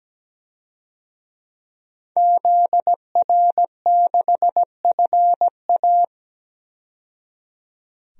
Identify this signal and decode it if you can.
Morse code, heard as a keyed tone, about 17 wpm, 700 Hz. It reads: ZR6FA
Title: DXCC-100 Call Signs - 5 Characters - 17wpm